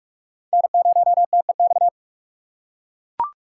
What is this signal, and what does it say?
Morse code, heard as a keyed tone, about 42 wpm, 700 Hz. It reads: N0TEX